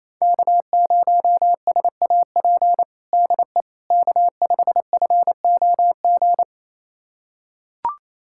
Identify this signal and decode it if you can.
Morse code, heard as a keyed tone, about 27 wpm, 700 Hz. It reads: K0SAP DE K5FOG
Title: USA Callsign DE Callsign - 5 Characters - 28wpm